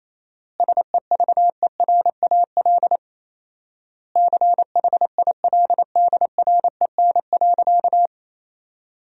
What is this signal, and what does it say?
Morse code, heard as a keyed tone, about 28 wpm, 700 Hz. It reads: SEVERAL CHILDREN.
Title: 4x Word-Spacing - Sentences from Top 400 Words - 28wpm